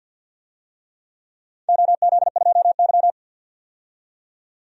Morse code, heard as a keyed tone, about 52 wpm, 700 Hz. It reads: OZ2Y
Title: DXCC-100 Call Signs - 4 Characters - 50wpm